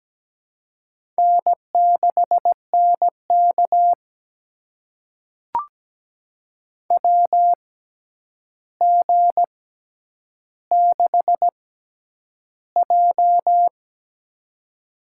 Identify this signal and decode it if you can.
Morse code, heard as a keyed tone, about 17 wpm, spaced as for 9 wpm, 700 Hz. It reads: N6NK WG6J
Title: USA Call Signs - 4 Characters - 6x Character Spacing - 17wpm